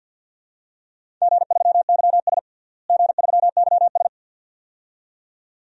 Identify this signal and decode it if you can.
Morse code, heard as a keyed tone, about 51 wpm, 700 Hz. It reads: G3YS G3YS
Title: DXCC-100 Call Signs - 4 Characters - Repeated 2x - 50wpm